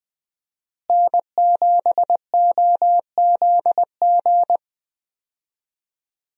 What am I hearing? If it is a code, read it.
Morse code, heard as a keyed tone, about 20 wpm, 700 Hz. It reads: N7OZG
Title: USA Call Signs - 5 Characters - Repeated 3x - 20wpm